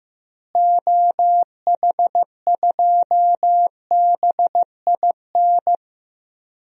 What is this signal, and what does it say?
Morse code, heard as a keyed tone, about 15 wpm, 700 Hz. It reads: OH2BIN
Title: DXCC-100 Call Signs - 6 Characters - Repeated 3x - 15wpm